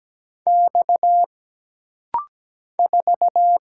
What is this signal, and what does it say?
Morse code, heard as a keyed tone, about 17 wpm, 700 Hz. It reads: X 4
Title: Single Letter-Number - Rapid-Fire 17wpm